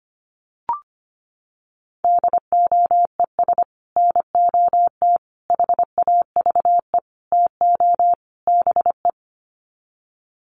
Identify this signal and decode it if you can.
Morse code, heard as a keyed tone, about 25 wpm, 700 Hz. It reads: DOES NOT HAVE TO BE
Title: Sets of 5 Words 25wpm